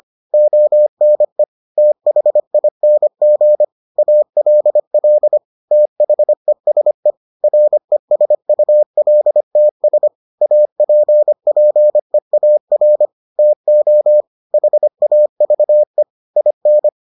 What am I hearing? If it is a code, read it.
Morse code, heard as a keyed tone, about 25 wpm, 600 Hz. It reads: ONE THING ALL THESE RESULTS APPEAR TO HAVE IN